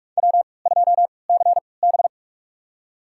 Morse code, heard as a keyed tone, about 46 wpm, spaced as for 26 wpm, 700 Hz. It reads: W2CB
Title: USA Call Signs - 4 Characters - 3x Character Spacing - 45wpm